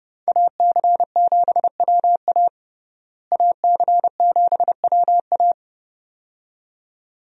Morse code, heard as a keyed tone, about 32 wpm, 700 Hz. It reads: AC7WA AC7WA
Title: USA Call Signs - 5 Characters - Repeated 2x - 30wpm